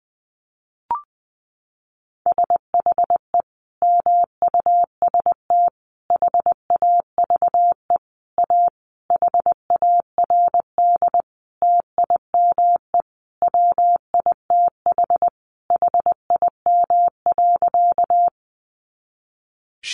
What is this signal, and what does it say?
Morse code, heard as a keyed tone, about 20 wpm, 700 Hz. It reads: SHE MUST HAVE A HARD TIME WITH HIM.
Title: Sentences from Top 300 Words 20wpm